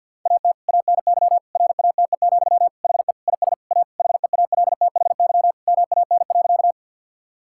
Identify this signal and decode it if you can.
Morse code, heard as a keyed tone, about 52 wpm, 700 Hz. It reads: AT ANY RATE, HE IS A HEALTHY MAN.
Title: Sentences from Top 1000 Words - Review Part 5 - 50wpm